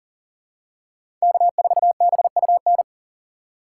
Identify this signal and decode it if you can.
Morse code, heard as a keyed tone, about 39 wpm, 700 Hz. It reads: K4BUN
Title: USA Call Signs - 3x Character Spacing - 40wpm